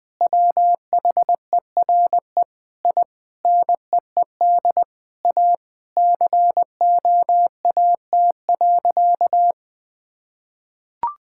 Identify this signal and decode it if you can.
Morse code, heard as a keyed tone, about 21 wpm, 700 Hz. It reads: WHERE I NEED A COAT.
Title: Sentences with 4-Word Phrases - Spoken Beginning - 20wpm